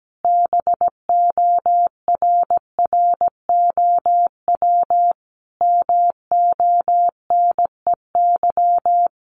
Morse code, heard as a keyed tone, about 17 wpm, 700 Hz. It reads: BORROW MONEY
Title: Sentences from Top 1500 Words 17wpm